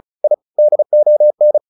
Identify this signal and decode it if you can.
Morse code, heard as a keyed tone, about 34 wpm, 600 Hz. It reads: I DON
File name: the old man and the sea / the old man and the sea 1 35wpm